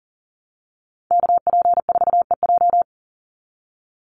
Morse code, heard as a keyed tone, about 40 wpm, 700 Hz. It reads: KP4EJ
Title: Call Signs - 40wpm